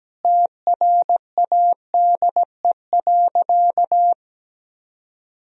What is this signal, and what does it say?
Morse code, heard as a keyed tone, about 17 wpm, 700 Hz. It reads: TRADE.
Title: Sentences from Top 1000 Words - Review Part 4 - 17wpm